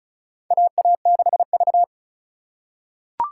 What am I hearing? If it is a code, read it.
Morse code, heard as a keyed tone, about 36 wpm, 700 Hz. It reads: AA6V
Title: USA Call Signs - 4x Character Spacing - 35wpm